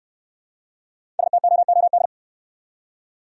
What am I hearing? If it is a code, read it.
Morse code, heard as a keyed tone, about 68 wpm, 700 Hz. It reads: STOOD